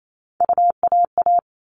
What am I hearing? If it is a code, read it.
Morse code, heard as a keyed tone, about 28 wpm, 700 Hz. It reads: UAA
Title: All Cut Numbers - 3-Digit 28wpm